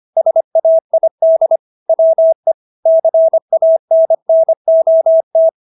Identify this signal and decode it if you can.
Morse code, heard as a keyed tone, about 25 wpm, 650 Hz. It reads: SAID WE CANNOT